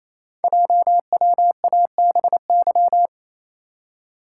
Morse code, heard as a keyed tone, about 27 wpm, 700 Hz. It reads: JWABY
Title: Code Groups - 5 Letters - 28wpm